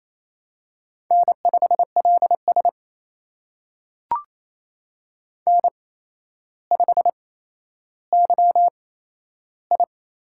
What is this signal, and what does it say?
Morse code, heard as a keyed tone, about 27 wpm, spaced as for 12 wpm, 700 Hz. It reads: N5LS N5YI
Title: USA Call Signs - 4 Characters - 8x Character Spacing - 28wpm